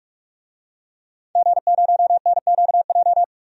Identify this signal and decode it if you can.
Morse code, heard as a keyed tone, about 43 wpm, 700 Hz. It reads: G0NQJ
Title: DXCC-100 Call Signs - 5 Characters - 45wpm